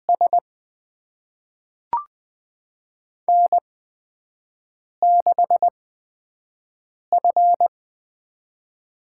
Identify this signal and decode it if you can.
Morse code, heard as a keyed tone, about 20 wpm, spaced as for 5 wpm, 700 Hz. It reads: S N6F
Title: USA Call Signs - 5 Characters - 8x Character Spacing - 20wpm